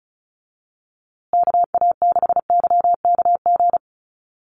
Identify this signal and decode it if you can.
Morse code, heard as a keyed tone, about 34 wpm, 700 Hz. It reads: KA6YKG
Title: USA Call Signs Repeated 2x 35wpm